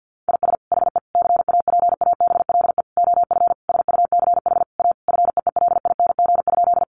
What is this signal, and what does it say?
Morse code, heard as a keyed tone, about 50 wpm, 700 Hz. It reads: IS HE CAPABLE OF SUCH A FEELING?
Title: Sentences from Top 1200 Words 50wpm